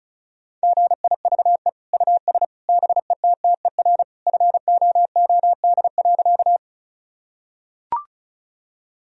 Morse code, heard as a keyed tone, about 34 wpm, 700 Hz. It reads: GIVE US BETTER FOOD.
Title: Sentences from Top 1000 Words - Review Part 1 - 35wpm